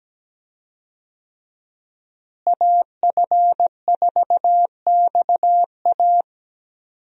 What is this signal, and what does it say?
Morse code, heard as a keyed tone, about 17 wpm, 700 Hz. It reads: AF4XA
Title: USA Call Signs - 5 Characters - Repeated 2x - 17wpm